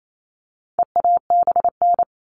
Morse code, heard as a keyed tone, about 28 wpm, 700 Hz. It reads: EABN
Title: All Cut Numbers - 4-Digit 28wpm